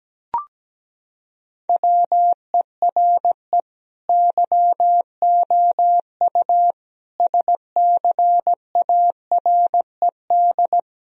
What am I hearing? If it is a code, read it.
Morse code, heard as a keyed tone, about 17 wpm, 700 Hz. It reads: WERE YOU SCARED
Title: Sentences from Top 1000 Words - Review Part 4 - 17wpm